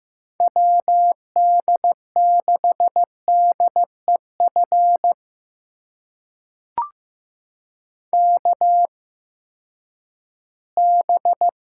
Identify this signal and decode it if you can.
Morse code, heard as a keyed tone, about 15 wpm, spaced as for 10 wpm, 700 Hz. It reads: WD6DEF KB
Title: USA Call Signs - 6 Characters - 8x Character Spacing - 15wpm